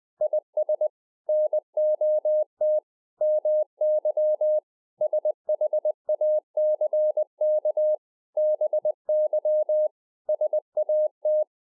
Morse code, heard as a keyed tone, about 20 wpm, 610 Hz. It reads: IS NOT MY SHACK BY SAT